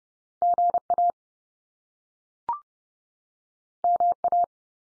Morse code, heard as a keyed tone, about 30 wpm, 700 Hz. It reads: GA MA
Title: US States 30WPM